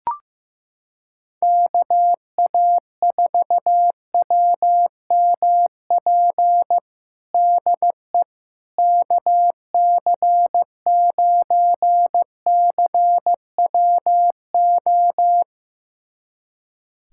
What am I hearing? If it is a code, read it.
Morse code, heard as a keyed tone, about 16 wpm, 700 Hz. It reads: KA4WMP DE KC9CWO
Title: USA Callsign DE Callsign - 6 Characters - 15wpm